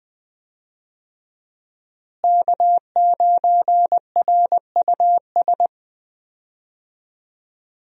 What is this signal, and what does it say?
Morse code, heard as a keyed tone, about 21 wpm, 700 Hz. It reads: K9RUS
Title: USA Call Signs - 5 Characters - Repeated 3x - 20wpm